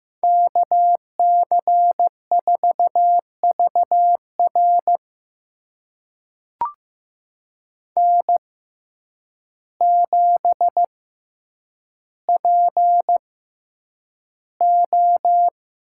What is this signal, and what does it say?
Morse code, heard as a keyed tone, about 15 wpm, spaced as for 8 wpm, 700 Hz. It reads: KC4VR N7PO
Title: USA Call Signs - 6x Character Spacing - 15wpm